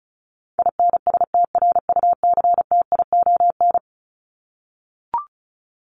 Morse code, heard as a keyed tone, about 34 wpm, 700 Hz. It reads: INSTRUCTION
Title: Top 1751-2000 Words 35wpm